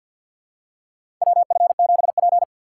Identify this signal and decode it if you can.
Morse code, heard as a keyed tone, about 50 wpm, 700 Hz. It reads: WF7P
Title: USA Call Signs - 4 Characters - 6x Character Spacing - 50wpm